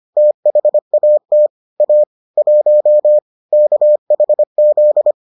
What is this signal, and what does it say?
Morse code, heard as a keyed tone, about 25 wpm, 600 Hz. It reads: THAT A 1 KHZ